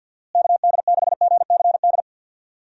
Morse code, heard as a keyed tone, about 53 wpm, 700 Hz. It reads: KD6GCD
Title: USA Call Signs - 6 Characters - Repeated 2x - 50wpm